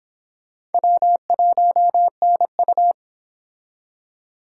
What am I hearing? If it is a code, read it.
Morse code, heard as a keyed tone, about 32 wpm, 700 Hz. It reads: W1NU